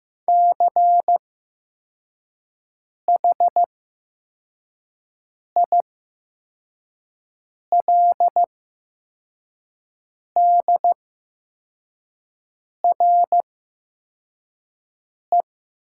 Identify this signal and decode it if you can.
Morse code, heard as a keyed tone, about 15 wpm, spaced as for 4 wpm, 700 Hz. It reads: CHILDRE
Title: 8x Character Spacing - Top 500 Words - 15wpm